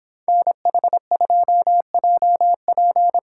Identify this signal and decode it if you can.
Morse code, heard as a keyed tone, about 26 wpm, 700 Hz. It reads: NH2JP
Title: Speed Racing - USA 2x2 Call Signs 22wpm